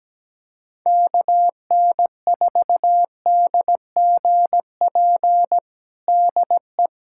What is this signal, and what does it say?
Morse code, heard as a keyed tone, about 17 wpm, 700 Hz. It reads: KN4DGP DE
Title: USA Callsign DE Callsign - 6 Characters - 17wpm